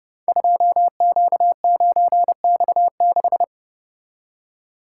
Morse code, heard as a keyed tone, about 30 wpm, 700 Hz. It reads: 2Q9X6
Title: Code Groups - 5 Letter:Numbers - 30wpm